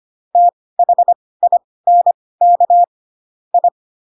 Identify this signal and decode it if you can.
Morse code, heard as a keyed tone, about 25 wpm, spaced as for 18 wpm, 700 Hz. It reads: THINK I
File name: Pinocchio / Pinocchio ch2 12 25 ik1hge